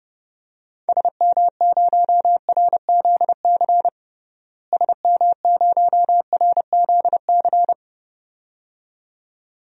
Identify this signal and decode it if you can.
Morse code, heard as a keyed tone, about 30 wpm, 700 Hz. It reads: SM0RZC SM0RZC
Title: DXCC-100 Call Signs - 6 Characters - Repeated 2x - 30wpm